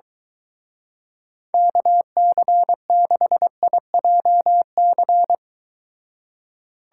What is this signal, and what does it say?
Morse code, heard as a keyed tone, about 23 wpm, 700 Hz. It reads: KC6IJC